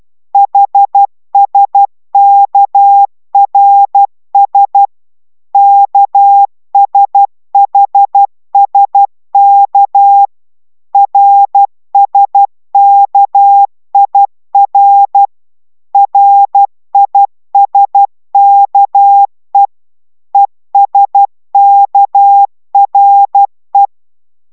Morse code, heard as a keyed tone, about 12 wpm, 800 Hz. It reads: HSKRS KSHSK RSKIR RISKE ESKRE